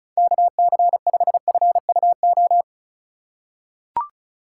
Morse code, heard as a keyed tone, about 35 wpm, 700 Hz. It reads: KC5FUO
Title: USA Call Signs - 6 Characters - 6x Character Spacing - 35wpm